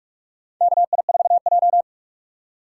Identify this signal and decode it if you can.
Morse code, heard as a keyed tone, about 47 wpm, 700 Hz. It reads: KI4J
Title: USA Call Signs - 4 Characters - 6x Character Spacing - 45wpm